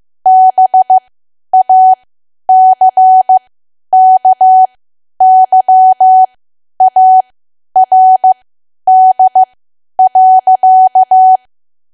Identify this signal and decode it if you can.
Morse code, heard as a keyed tone, about 15 wpm, spaced as for 10 wpm, 750 Hz. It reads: BACKYARD.
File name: cwa / W1AW-Level2 / 10WPM